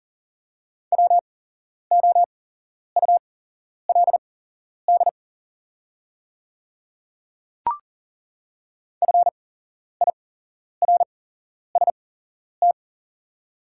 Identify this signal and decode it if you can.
Morse code, heard as a keyed tone, about 40 wpm, spaced as for 11 wpm, 700 Hz. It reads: WOULD FIRST
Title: 8x Character Spacing - Top 100 Words - 40wpm